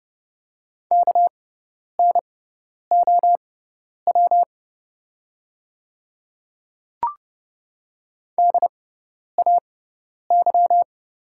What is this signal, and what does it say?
Morse code, heard as a keyed tone, about 30 wpm, spaced as for 10 wpm, 700 Hz. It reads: KNOW DAY
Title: 6x Character Spacing - Top 100 Words - 30wpm